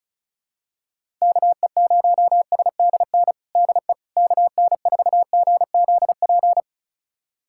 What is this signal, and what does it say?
Morse code, heard as a keyed tone, about 36 wpm, 700 Hz. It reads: KE0SDN DE KN4GZP
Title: USA Callsign DE Callsign - 6 Characters - 35wpm